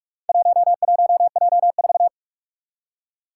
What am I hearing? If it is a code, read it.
Morse code, heard as a keyed tone, about 43 wpm, 700 Hz. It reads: 11J4
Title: Four Challenging Letter-Number Combinations 45wpm